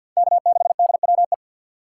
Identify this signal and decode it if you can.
Morse code, heard as a keyed tone, about 55 wpm, 700 Hz. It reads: K6DWE
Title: Speed Racing - USA 1x3 Call Signs 50wpm